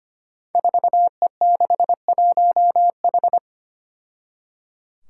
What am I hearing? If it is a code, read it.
Morse code, heard as a keyed tone, about 25 wpm, 700 Hz. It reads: 4E61H
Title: Five Challenging Letter-Number Combinations 25wpm